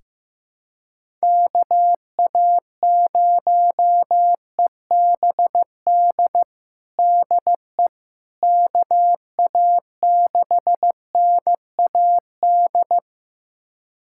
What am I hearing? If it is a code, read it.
Morse code, heard as a keyed tone, about 16 wpm, 700 Hz. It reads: KA0EBD DE KA6NAD
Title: USA Callsign DE Callsign - 6 Characters - 15wpm